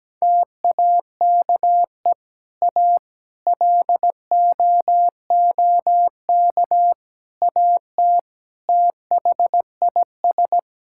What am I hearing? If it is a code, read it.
Morse code, heard as a keyed tone, about 17 wpm, 700 Hz. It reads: TAKE A LOOK AT THIS
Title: Sentences from Top 1000 Words - Review Part 5 - 17wpm